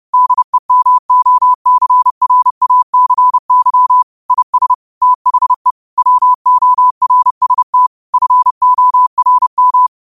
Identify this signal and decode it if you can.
Morse code, heard as a keyed tone, about 30 wpm, 1000 Hz. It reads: DEMOCRACY IS THE WORST FORM
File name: qotdfiles / 30wpm